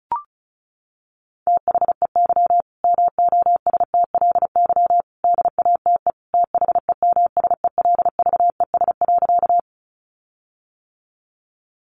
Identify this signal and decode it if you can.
Morse code, heard as a keyed tone, about 34 wpm, 700 Hz. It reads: THEY MOSTLY DATE THEMSELVES.
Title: Sentences from Top 1300 Words 35wpm